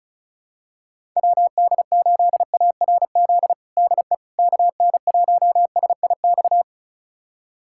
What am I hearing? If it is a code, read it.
Morse code, heard as a keyed tone, about 36 wpm, 700 Hz. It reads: WD8ARZ DE KN1SIX
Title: USA Callsign DE Callsign - 6 Characters - 35wpm